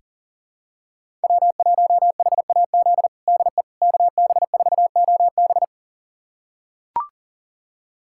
Morse code, heard as a keyed tone, about 37 wpm, 700 Hz. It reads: W1HAZ DE KB4OB
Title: USA Callsign DE Callsign - 5 Characters - 40wpm